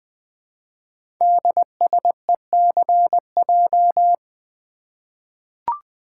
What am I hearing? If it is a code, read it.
Morse code, heard as a keyed tone, about 20 wpm, 700 Hz. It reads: DSECJ
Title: Code Groups - 5 Letters - 20wpm